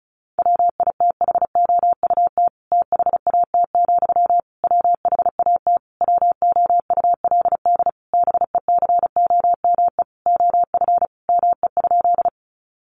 Morse code, heard as a keyed tone, about 34 wpm, 700 Hz. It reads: WITHOUT THAT, WHAT WOULD BECOME OF ME?
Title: Sentences from Top 600 Words 35wpm